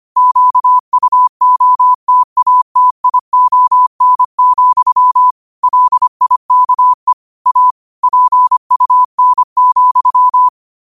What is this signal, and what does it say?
Morse code, heard as a keyed tone, about 25 wpm, 1000 Hz. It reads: QUOTATION, LIKE A PUN,
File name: qotdfiles / 25wpm